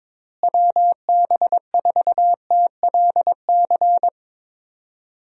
Morse code, heard as a keyed tone, about 21 wpm, 700 Hz. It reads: WB4TLC
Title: USA Call Signs - 6 Characters - Repeated 2x - 22wpm